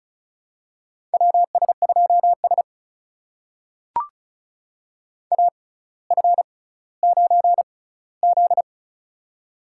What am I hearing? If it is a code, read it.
Morse code, heard as a keyed tone, about 36 wpm, spaced as for 18 wpm, 700 Hz. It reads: WS2S AF9Z
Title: USA Call Signs - 4 Characters - 6x Character Spacing - 35wpm